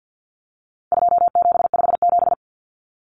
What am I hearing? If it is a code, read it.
Morse code, heard as a keyed tone, about 50 wpm, 700 Hz. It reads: 2757
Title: Common Cut Numbers - 4-Digit 50wpm